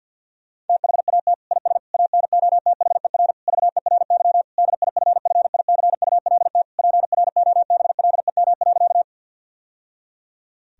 Morse code, heard as a keyed tone, about 50 wpm, 700 Hz. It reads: THAT IS ANOTHER VERY DIFFICULT PROBLEM.
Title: Sentences from Top 1000 Words - Review Part 2 - 50wpm